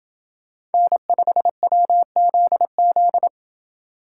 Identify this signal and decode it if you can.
Morse code, heard as a keyed tone, about 29 wpm, 700 Hz. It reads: N5WZZ